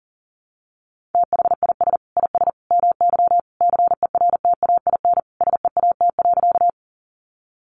Morse code, heard as a keyed tone, about 40 wpm, 700 Hz. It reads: THIS IS MY CERTAIN SEAT.
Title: Sentences from Top 1500 Words 40wpm